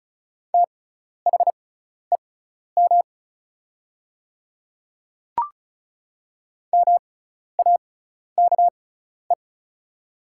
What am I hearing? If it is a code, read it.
Morse code, heard as a keyed tone, about 36 wpm, spaced as for 12 wpm, 700 Hz. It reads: THEM MAKE
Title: 6x Character Spacing - Top 100 Words - 35wpm